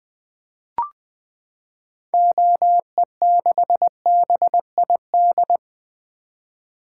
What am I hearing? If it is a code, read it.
Morse code, heard as a keyed tone, about 20 wpm, 700 Hz. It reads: OE6BID
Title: 2023 SOTA Activator Call Signs - 20wpm